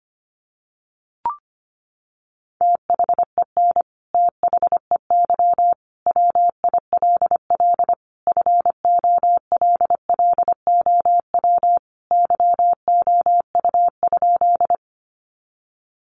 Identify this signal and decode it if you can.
Morse code, heard as a keyed tone, about 25 wpm, 700 Hz. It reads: THEN THEY WILL FOLLOW YOU?
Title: Sentences from Top 200 Words 25wpm